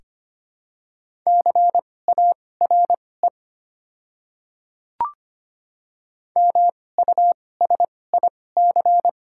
Morse code, heard as a keyed tone, about 25 wpm, spaced as for 18 wpm, 700 Hz. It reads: CARE MUSIC
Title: 2x Character Spacing - Top 300 Words - 25wpm